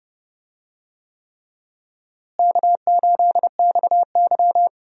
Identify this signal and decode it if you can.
Morse code, heard as a keyed tone, about 30 wpm, 700 Hz. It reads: K8XY
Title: USA Call Signs - 4 Characters - Repeated 3x - 30wpm